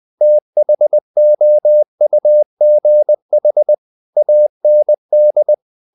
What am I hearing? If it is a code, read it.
Morse code, heard as a keyed tone, about 20 wpm, 600 Hz. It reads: THOUGH AND
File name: the old man and the sea / the old man and the sea part 1 20wpm